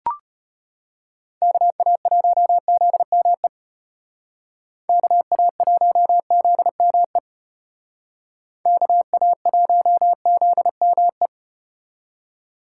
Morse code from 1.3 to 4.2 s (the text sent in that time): KA1ZME